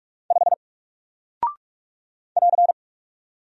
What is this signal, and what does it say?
Morse code, heard as a keyed tone, about 44 wpm, 700 Hz. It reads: 5 <AR>
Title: Single Character (Letter:Number:Punctuation:Prosign) 45wpm